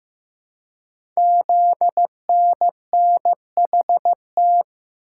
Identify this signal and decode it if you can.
Morse code, heard as a keyed tone, about 15 wpm, 700 Hz. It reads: ZNNHT
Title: Code Groups - 5 Letters - 15wpm